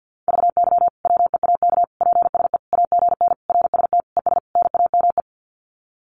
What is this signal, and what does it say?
Morse code, heard as a keyed tone, about 50 wpm, 700 Hz. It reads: VY WEAK PSE AGN RST ES NAME
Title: Common QSO Phrases 50wpm